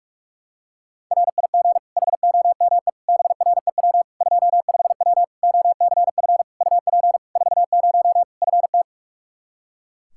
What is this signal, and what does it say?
Morse code, heard as a keyed tone, about 41 wpm, 700 Hz. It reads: RIG HOME BREW 25W OCF UP 40 FT
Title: Practice QSO - Exchange 2 - Info - 45wpm